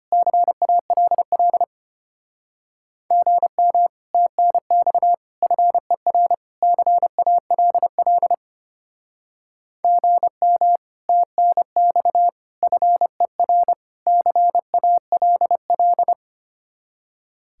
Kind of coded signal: Morse code

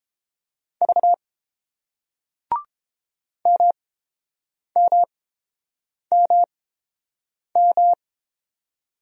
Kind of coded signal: Morse code

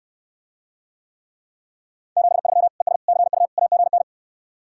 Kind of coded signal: Morse code

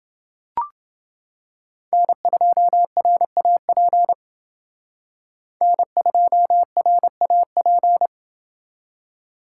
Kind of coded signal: Morse code

